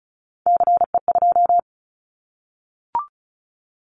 Morse code, heard as a keyed tone, about 36 wpm, 700 Hz. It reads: CE2
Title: Callsign Prefixes for Top 100 DXCC - 35wpm